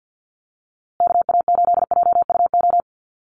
Morse code, heard as a keyed tone, about 48 wpm, 700 Hz. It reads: KA8JUO
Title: USA Call Signs Repeated 2x 50wpm